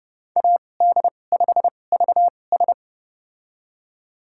Morse code, heard as a keyed tone, about 31 wpm, spaced as for 22 wpm, 700 Hz. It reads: AD5VS
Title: USA Call Signs - 2x Character Spacing - 30wpm